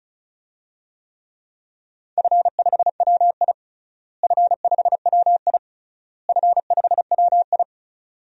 Morse code, heard as a keyed tone, about 36 wpm, 700 Hz. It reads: F5WI F5WI F5WI
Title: DXCC-100 Call Signs - 4 Characters - Repeated 3x - 35wpm